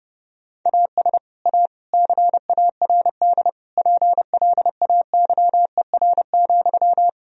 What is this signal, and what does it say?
Morse code, heard as a keyed tone, about 32 wpm, 700 Hz. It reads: AS A CARD PLAYER,
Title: Sentences from Top 1000 Words - Review Part 4 - 30wpm